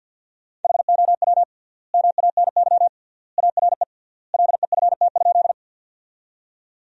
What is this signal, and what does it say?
Morse code, heard as a keyed tone, about 50 wpm, 700 Hz. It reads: HOW MANY ARE LEFT?
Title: 3x Word-Spacing - Sentences from Top 300 Words - 50wpm